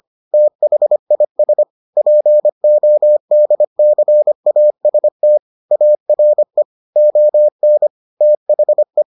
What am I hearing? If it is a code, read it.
Morse code, heard as a keyed tone, about 25 wpm, 600 Hz. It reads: THIS PODCAST ARE ON THE